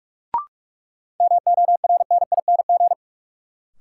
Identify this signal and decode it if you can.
Morse code, heard as a keyed tone, about 46 wpm, 700 Hz. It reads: MORNING
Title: Sentences from Top 400 Words - New Word Drill - 45wpm